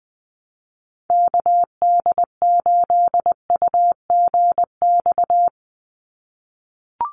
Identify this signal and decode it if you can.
Morse code, heard as a keyed tone, about 20 wpm, 700 Hz. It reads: KD8UGX
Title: USA Call Signs 20wpm